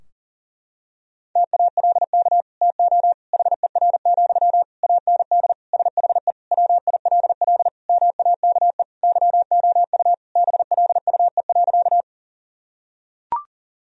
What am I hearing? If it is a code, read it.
Morse code, heard as a keyed tone, about 37 wpm, 700 Hz. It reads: TALK TO HER, AND SHE WILL MAKE YOU BLUE.
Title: Sentences from Top 1000 Words - Review Part 3 - 40wpm